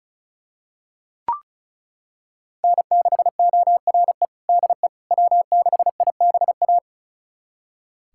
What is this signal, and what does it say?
Morse code, heard as a keyed tone, about 36 wpm, 700 Hz. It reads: N6ORE DE W6IBA
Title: USA Callsign DE Callsign - 5 Characters - 35wpm